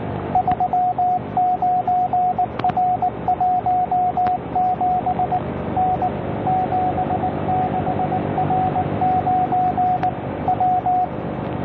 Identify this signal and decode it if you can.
Morse code, heard as a keyed tone, about 19 wpm, 730 Hz, with radio noise. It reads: 39R17 N 76R9 W